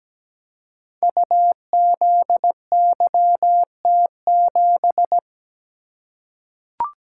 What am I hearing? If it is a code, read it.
Morse code, heard as a keyed tone, about 17 wpm, 700 Hz. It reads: UZYT7
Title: Code Groups - 5 Letter:Numbers - 17wpm